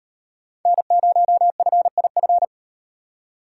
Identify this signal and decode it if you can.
Morse code, heard as a keyed tone, about 37 wpm, 700 Hz. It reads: N0FIF